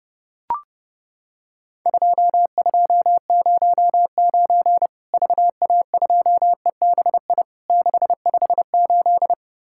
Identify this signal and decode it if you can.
Morse code, heard as a keyed tone, about 30 wpm, 700 Hz. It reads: 2209 VA2EBI 658
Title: Worked All Europe DX Contest - QTC Exchange - 30wpm